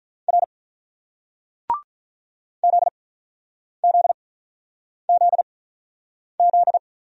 Morse code, heard as a keyed tone, about 48 wpm, spaced as for 9 wpm, 700 Hz. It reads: R ZZZZ